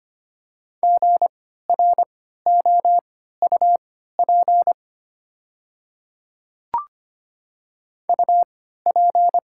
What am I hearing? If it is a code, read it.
Morse code, heard as a keyed tone, about 25 wpm, spaced as for 14 wpm, 690 Hz. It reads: GROUP UP
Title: 3x Character Spacing - Top 500 Words - 25wpm